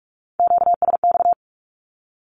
Morse code, heard as a keyed tone, about 44 wpm, 700 Hz. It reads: QSX
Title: Q-Codes 45wpm